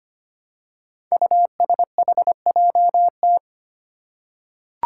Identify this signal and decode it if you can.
Morse code, heard as a keyed tone, about 25 wpm, 700 Hz. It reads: USHJT
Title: Five Challenging Letter-Number Combinations 25wpm